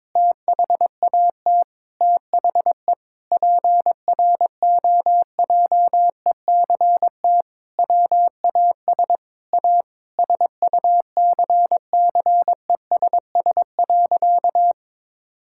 Morse code, heard as a keyed tone, about 22 wpm, 700 Hz. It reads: THAT THE PROJECT WAS A SUCCESS.
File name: practice / 22wpm / Sentences with 8-Word Phrases - 22wpm